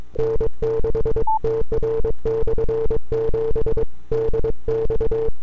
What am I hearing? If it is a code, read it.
Morse code, heard as a keyed tone, about 22 wpm, 440 Hz. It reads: N6TR/7 DX
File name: n6tr / 12Oct2009-160